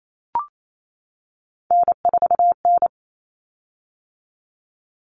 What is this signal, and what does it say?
Morse code, heard as a keyed tone, about 28 wpm, 700 Hz. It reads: N4N
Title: Common Cut Numbers - 3-Digit 28wpm